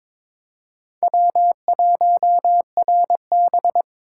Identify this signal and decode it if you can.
Morse code, heard as a keyed tone, about 26 wpm, 700 Hz. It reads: W1RB
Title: Speed Racing - USA 1x2 Call Signs 22wpm